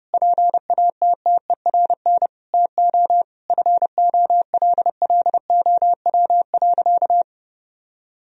Morse code, heard as a keyed tone, about 30 wpm, 700 Hz. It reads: PATTERN TO FOLLOW.
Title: Sentences from Top 1000 Words - Review Part 2 - 30wpm